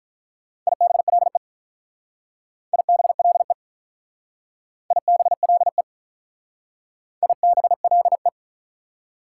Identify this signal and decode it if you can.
Morse code, heard as a keyed tone, about 47 wpm, 700 Hz. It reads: IBLE IBLE IBLE IBLE